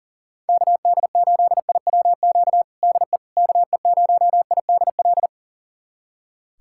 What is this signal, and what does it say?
Morse code, heard as a keyed tone, about 40 wpm, 700 Hz. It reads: KD8IWQ DE KE0IDL
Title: USA Callsign DE Callsign - 6 Characters - 40wpm